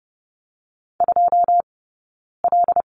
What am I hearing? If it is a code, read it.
Morse code, heard as a keyed tone, about 30 wpm, 700 Hz. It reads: 2 L
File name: Lesson 18d - Letter m - Two Characters 30wpm